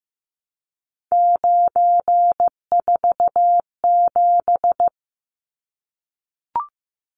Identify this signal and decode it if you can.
Morse code, heard as a keyed tone, about 15 wpm, 700 Hz. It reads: 947
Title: Numbers - 3-Digit 15wpm